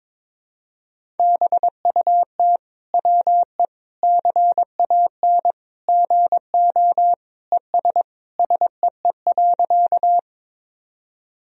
Morse code, heard as a keyed tone, about 21 wpm, 700 Hz. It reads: BUT WE CAN GO ES SEE.
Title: Sentences from Top 100 Words and Abbreviations 22wpm